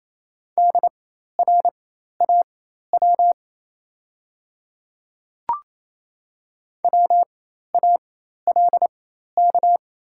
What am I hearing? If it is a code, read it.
Morse code, heard as a keyed tone, about 28 wpm, spaced as for 13 wpm, 700 Hz. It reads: DRAW WALK
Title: 4x Character Spacing - Top 300 Words - 28wpm